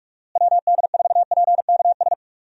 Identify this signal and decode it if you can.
Morse code, heard as a keyed tone, about 46 wpm, 700 Hz. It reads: WD4PKS
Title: USA Call Signs - 6 Characters - Repeated 3x - 45wpm